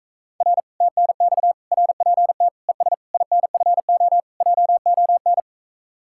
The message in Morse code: R TNX RPT ES INFO JON

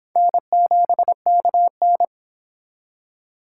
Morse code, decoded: N7KN

